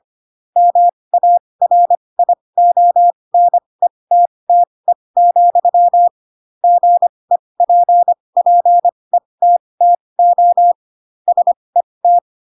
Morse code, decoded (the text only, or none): MARIONETTE, GEPPETTO SET